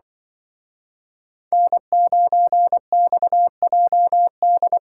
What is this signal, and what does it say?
Morse code, heard as a keyed tone, about 24 wpm, 700 Hz. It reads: N9XJD
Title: Speed Racing - USA 1x3 Call Signs 20wpm